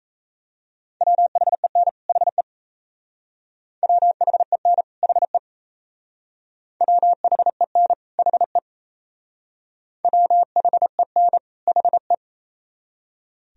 Morse code, heard as a keyed tone, about 35 wpm, 700 Hz. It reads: WHEN HE WHEN HE WHEN HE WHEN HE